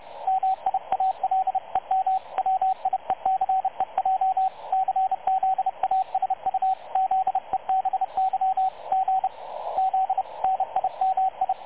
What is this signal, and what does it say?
Morse code, heard as a keyed tone, about 30 wpm, 760 Hz, with radio noise. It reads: MIALEM WIECEJ CZASU ZEBY G Z NIMI
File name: dlazaawansowanych / nagraniazpasma / websdr recording 3567.0kHz